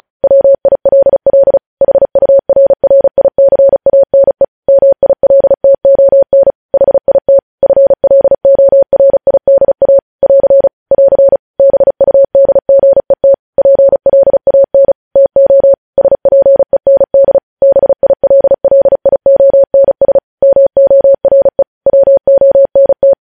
Morse code, heard as a keyed tone, about 36 wpm, 550 Hz. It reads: WILL HURRICANE MILTON HIT FLORIDA <AR> <AR> BUDGET PLAN TO SPEND BILLIONS MORE WONT